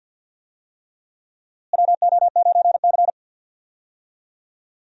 Morse code, heard as a keyed tone, about 48 wpm, 700 Hz. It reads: WO9C
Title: USA Call Signs - 4 Characters - Repeated 3x - 50wpm